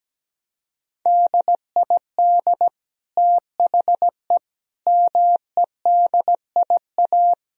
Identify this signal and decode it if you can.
Morse code, heard as a keyed tone, about 17 wpm, 700 Hz. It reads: DID THE MEDIA
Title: Sentences from Top 1000 Words - Review Part 4 - 17wpm